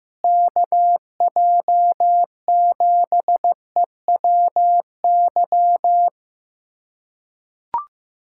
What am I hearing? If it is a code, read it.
Morse code, heard as a keyed tone, about 15 wpm, 700 Hz. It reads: KJ7EWY
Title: USA Callsign DE Callsign 15wpm